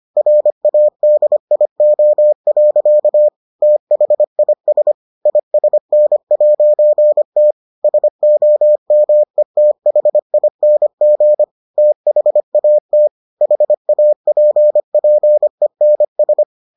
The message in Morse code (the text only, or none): RADIO. THIS ISN'T SOMETHING THAT HAPPENS